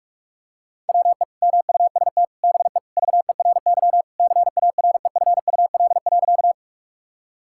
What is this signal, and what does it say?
Morse code, heard as a keyed tone, about 41 wpm, 700 Hz. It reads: WE MUST BE VERY CAREFUL.
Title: Sentences from Top 1600 Words 45wpm